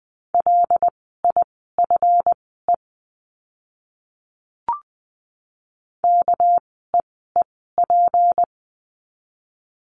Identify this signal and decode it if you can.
Morse code, heard as a keyed tone, about 20 wpm, spaced as for 15 wpm, 700 Hz. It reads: LIFE KEEP
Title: 230 Lesson 37f - Letter x - Words 20wpm